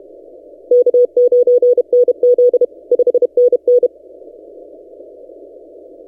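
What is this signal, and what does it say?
Morse code, heard as a keyed tone, about 32 wpm, 490 Hz. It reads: K9NZ 5NN
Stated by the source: Morse from DXpedition station A35X on 12 m, posted by N6ML.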